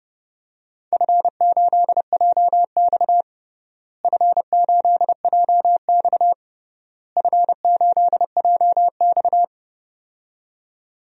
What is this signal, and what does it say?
Morse code, heard as a keyed tone, about 32 wpm, 700 Hz. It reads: F8JX F8JX F8JX
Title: DXCC-100 Call Signs - 4 Characters - Repeated 3x - 30wpm